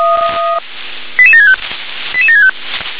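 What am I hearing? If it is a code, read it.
Five-tone selective call (ZVEI), heard as quick runs of five tones, tones of about 70 ms. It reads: 88765 88765
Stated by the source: ZVEI five-tone selective call 88707 received on an SDR.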